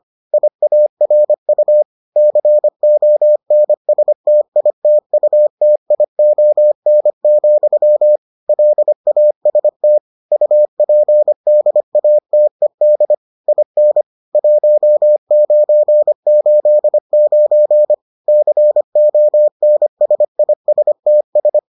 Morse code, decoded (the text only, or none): IARU CONSTITUTION, LAST UPDATED IN 1989 CONSISTS